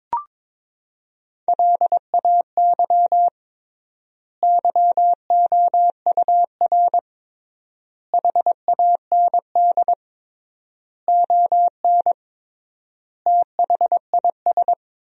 Morse code, heard as a keyed tone, about 22 wpm, 700 Hz. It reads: LAY YOUR HAND ON THIS
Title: 3x Word-Spacing - Sentences from Top 400 Words - 22wpm